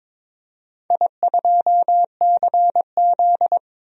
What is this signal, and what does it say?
Morse code, heard as a keyed tone, about 22 wpm, 700 Hz. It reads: I2CZ
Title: DXCC-100 Call Signs - 4 Characters - 22wpm